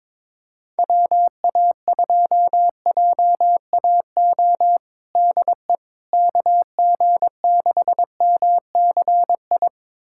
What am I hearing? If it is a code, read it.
Morse code, heard as a keyed tone, about 22 wpm, 700 Hz. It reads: WA2JAO DE KG6MCI
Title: USA Callsign DE Callsign 22wpm